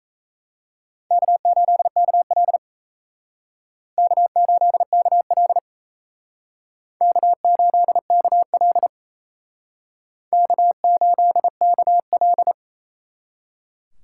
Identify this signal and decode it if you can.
Morse code, heard as a keyed tone, about 38 wpm, 700 Hz. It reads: K8KL K8KL K8KL K8KL